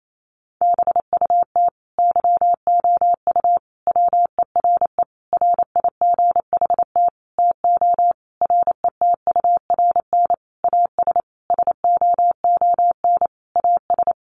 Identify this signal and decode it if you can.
Morse code, heard as a keyed tone, about 28 wpm, 700 Hz. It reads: BUT YOU WERE RIGHT TO RETURN AS SOON AS
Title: Sentences from Top 800 Words 28wpm